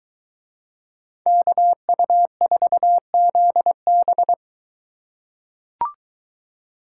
Morse code, heard as a keyed tone, about 21 wpm, 700 Hz. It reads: KU4ZB